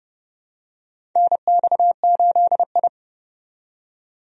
Morse code, heard as a keyed tone, about 30 wpm, 700 Hz. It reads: NX8I
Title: USA Call Signs - 4 Characters - 30wpm